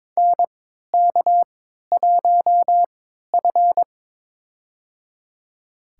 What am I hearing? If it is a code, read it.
Morse code, heard as a keyed tone, about 22 wpm, spaced as for 13 wpm, 700 Hz. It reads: NK1F
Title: USA Call Signs - 4 Characters - 3x Character Spacing - 22wpm